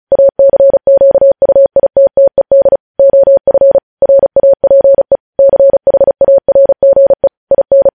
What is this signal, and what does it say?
Morse code, heard as a keyed tone, about 36 wpm, 550 Hz. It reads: ACQUITTED OF RAPE CHARGE IN
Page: News Headlines in Morse Code at 35 WPM